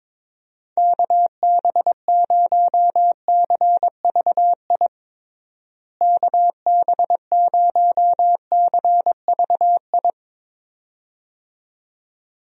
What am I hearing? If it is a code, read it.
Morse code, heard as a keyed tone, about 22 wpm, 700 Hz. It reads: KB0CVI KB0CVI
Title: USA Call Signs - 6 Characters - Repeated 2x - 22wpm